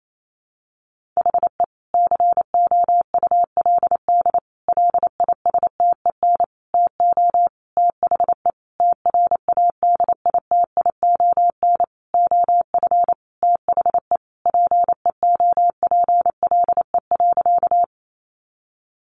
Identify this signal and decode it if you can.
Morse code, heard as a keyed tone, about 28 wpm, 700 Hz. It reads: HE COULD LISTEN TO THE TRADITION OF THE PEOPLE.
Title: Sentences from Top 1500 Words 28wpm